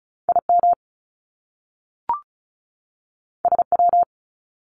Morse code, heard as a keyed tone, about 36 wpm, 700 Hz. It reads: IM SW
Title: Top Two Letter Combinations in Words 35wpm